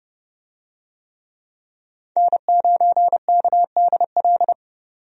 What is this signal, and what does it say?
Morse code, heard as a keyed tone, about 30 wpm, 700 Hz. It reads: N9KDL
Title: USA Call Signs - 5 Characters - 30wpm